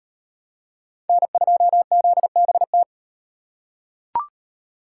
Morse code, heard as a keyed tone, about 38 wpm, 700 Hz. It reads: N2ZBT